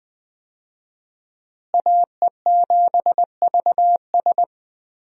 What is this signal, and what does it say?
Morse code, heard as a keyed tone, about 21 wpm, 700 Hz. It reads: AE7VS
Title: USA Call Signs - 5 Characters - Repeated 2x - 20wpm